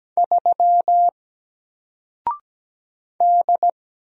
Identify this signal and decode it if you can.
Morse code, heard as a keyed tone, about 17 wpm, 700 Hz. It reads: 3 D
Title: Single Character (Letter:Number:Punctuation:Prosign) 17wpm